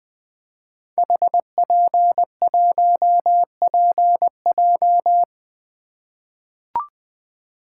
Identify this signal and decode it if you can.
Morse code, heard as a keyed tone, about 20 wpm, 700 Hz. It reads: HP1PJ
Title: Code Groups - 5 Letter:Numbers - 20wpm